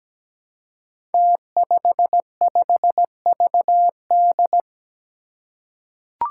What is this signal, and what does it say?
Morse code, heard as a keyed tone, about 17 wpm, 700 Hz. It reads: T55VD
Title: Five Challenging Letter-Number Combinations 17wpm